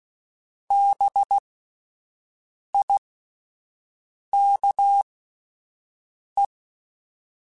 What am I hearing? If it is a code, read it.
Morse code, heard as a keyed tone, about 16 wpm, spaced as for 5 wpm, 800 Hz. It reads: BIKE